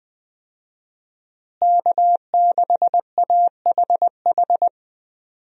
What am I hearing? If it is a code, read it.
Morse code, heard as a keyed tone, about 21 wpm, 700 Hz. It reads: K6AHH
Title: USA Call Signs - 5 Characters - Repeated 3x - 20wpm